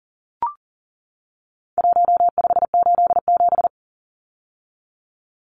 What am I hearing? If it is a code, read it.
Morse code, heard as a keyed tone, about 40 wpm, 700 Hz. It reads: 1587
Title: Numbers - 4-Digit 40wpm